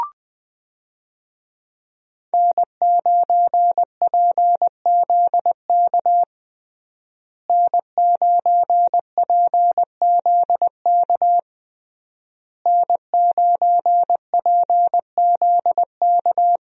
Morse code, heard as a keyed tone, about 20 wpm, 700 Hz. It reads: N9PZK N9PZK N9PZK
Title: USA Call Signs - 5 Characters - Repeated 3x - 20wpm